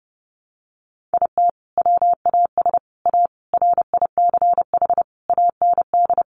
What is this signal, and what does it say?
Morse code, heard as a keyed tone, about 30 wpm, 700 Hz. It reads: IT WAS A RICH AND
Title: Sentences from Top 1100 Words 30wpm